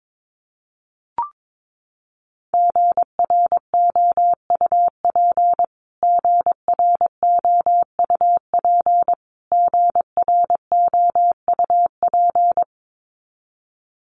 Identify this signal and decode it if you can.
Morse code, heard as a keyed tone, about 22 wpm, 700 Hz. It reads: GROUP GROUP GROUP
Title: Top 500 Words Repeated 22wpm